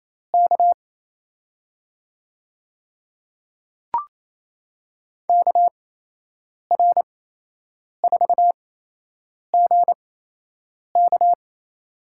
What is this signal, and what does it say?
Morse code, heard as a keyed tone, about 28 wpm, spaced as for 8 wpm, 700 Hz. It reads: K KR4GK
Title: USA Call Signs - 5 Characters - 8x Character Spacing - 28wpm